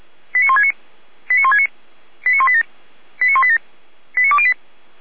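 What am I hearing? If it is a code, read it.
Five-tone selective call (ZVEI), heard as quick runs of five tones, tones of about 70 ms. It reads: 89169 89169 89187 89187 89208
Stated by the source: ZVEI five-tone selective call 89174, received by SDR.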